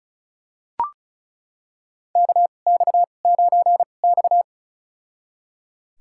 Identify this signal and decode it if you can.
Morse code, heard as a keyed tone, about 36 wpm, spaced as for 25 wpm, 700 Hz. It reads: KX9X
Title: USA Call Signs - 4 Characters - 2x Character Spacing - 35wpm